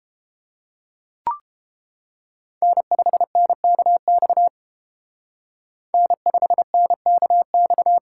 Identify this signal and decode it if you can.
Morse code, heard as a keyed tone, about 30 wpm, 700 Hz. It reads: N5NKX N5NKX